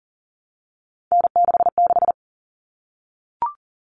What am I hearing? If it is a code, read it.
Morse code, heard as a keyed tone, about 40 wpm, 700 Hz. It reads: N66
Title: All Cut Numbers - 3-Digit 40wpm